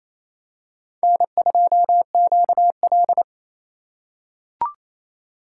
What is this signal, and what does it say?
Morse code, heard as a keyed tone, about 28 wpm, 700 Hz. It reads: N2QL
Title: USA Call Signs - 4 Characters - 2x Character Spacing - 28wpm